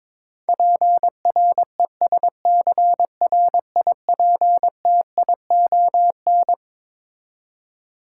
Morse code, heard as a keyed tone, about 22 wpm, 700 Hz. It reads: PRESCRIPTION
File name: practice / 22wpm / Twelve Letter Words - 22wpm